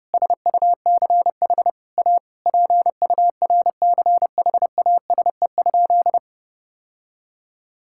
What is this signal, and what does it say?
Morse code, heard as a keyed tone, about 32 wpm, 700 Hz. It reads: SUCH A PURCHASE?
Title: Sentences from Top 1000 Words - Review Part 6 - 30wpm